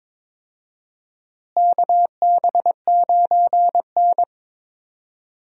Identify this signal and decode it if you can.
Morse code, heard as a keyed tone, about 22 wpm, 700 Hz. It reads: KB9N
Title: USA Call Signs - 4 Characters - Repeated 2x - 22wpm